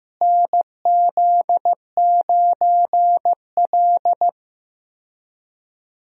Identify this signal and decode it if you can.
Morse code, heard as a keyed tone, about 16 wpm, 700 Hz. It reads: NZ9L
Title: USA Call Signs - 4 Characters - Repeated 3x - 15wpm